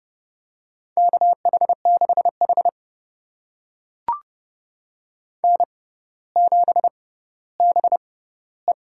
Morse code, heard as a keyed tone, about 30 wpm, spaced as for 15 wpm, 700 Hz. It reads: KH6H N7BE
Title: USA Call Signs - 4 Characters - 6x Character Spacing - 30wpm